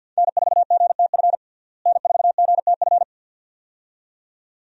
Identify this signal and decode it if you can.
Morse code, heard as a keyed tone, about 50 wpm, 700 Hz. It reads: N4GTF N4GTF
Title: USA Call Signs - 5 Characters - Repeated 2x - 50wpm